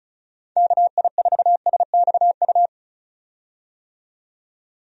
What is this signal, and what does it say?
Morse code, heard as a keyed tone, about 36 wpm, 700 Hz. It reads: KI4SXU
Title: USA Call Signs - 6 Characters - 35wpm